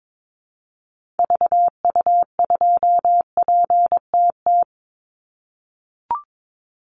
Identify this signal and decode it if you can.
Morse code, heard as a keyed tone, about 22 wpm, 700 Hz. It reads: VU2PTT
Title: Call Signs - 22wpm